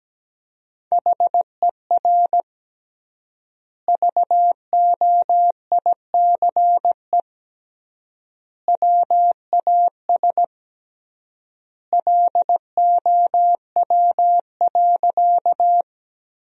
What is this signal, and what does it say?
Morse code, heard as a keyed tone, about 17 wpm, 700 Hz. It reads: HER VOICE WAS LOW.
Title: 3x Word-Spacing - Sentences from Top 500 Words - 17wpm